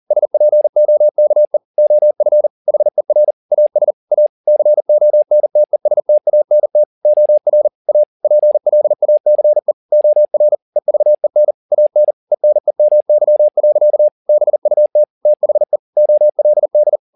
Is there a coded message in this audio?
Yes: Morse code from 0.1 to 17.0 s, heard as a keyed tone, about 40 wpm, 600 Hz. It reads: SPOKE OF HER AS A CONTESTANT OR A PLACE OR EVEN AN ENEMY. BUT THE OLD